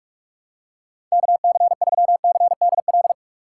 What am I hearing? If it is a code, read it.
Morse code, heard as a keyed tone, about 41 wpm, 700 Hz. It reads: KC3CDL
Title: USA Call Signs - 6 Characters - 4x Character Spacing - 45wpm